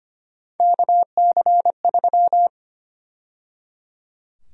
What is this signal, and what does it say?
Morse code, heard as a keyed tone, about 25 wpm, 700 Hz. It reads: KC3
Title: USA Call Sign Prefixes - 25wpm